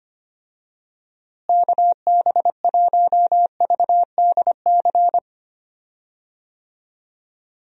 Morse code, heard as a keyed tone, about 25 wpm, 700 Hz. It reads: KB1VDC
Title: USA Call Signs - 6 Characters - Repeated 3x - 25wpm